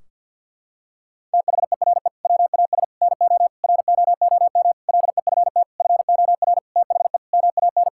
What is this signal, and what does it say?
Morse code, heard as a keyed tone, about 50 wpm, 700 Hz. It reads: THERE WAS NO ROOM LEFT FOR THE MAN
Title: Sentences from Top 1600 Words 50wpm